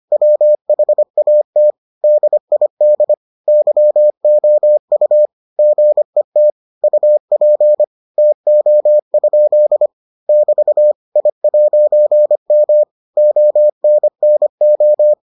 Morse code, heard as a keyed tone, about 25 wpm, 600 Hz. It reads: WHAT DID YOU GET UP TO? <BT> I'M ONNO